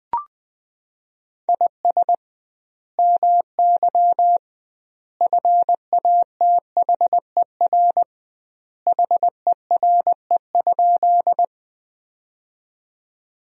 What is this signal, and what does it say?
Morse code, heard as a keyed tone, about 19 wpm, 700 Hz. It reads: IS MY FATHER HERE?
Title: 2x Word-Spacing - Sentences from Top 300 Words - 20wpm